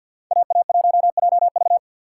Morse code, heard as a keyed tone, about 60 wpm, 700 Hz. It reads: AA1JV
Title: Speed Racing - USA 2x2 Call Signs 50wpm